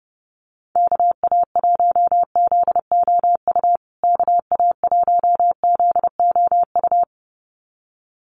Morse code, heard as a keyed tone, about 30 wpm, 700 Hz. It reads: KA1ZOU KA1ZOU
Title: USA Call Signs Repeated 2x 30wpm